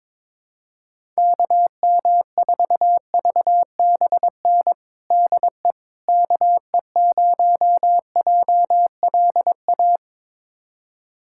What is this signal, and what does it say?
Morse code, heard as a keyed tone, about 22 wpm, 700 Hz. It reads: KM4VBN DE KE0JLA
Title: USA Callsign DE Callsign - 6 Characters - 22wpm